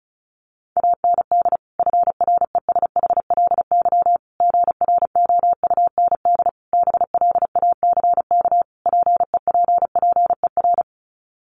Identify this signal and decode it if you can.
Morse code, heard as a keyed tone, about 34 wpm, 700 Hz. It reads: AND FRESHLY GROUND BLACK PEPPER
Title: Sets of 5 Words Encore 35wpm